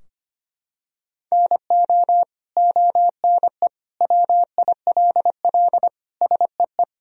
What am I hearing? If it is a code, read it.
Morse code, heard as a keyed tone, about 25 wpm, 700 Hz. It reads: NO ONE WILL SEE
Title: Sentences from Top 1000 Words - Review Part 3 - 25wpm